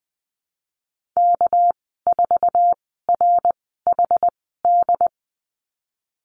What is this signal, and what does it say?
Morse code, heard as a keyed tone, about 20 wpm, spaced as for 15 wpm, 700 Hz. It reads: K4RHD
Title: 200 Lesson 32g - Letter k - Call Signs 20wpm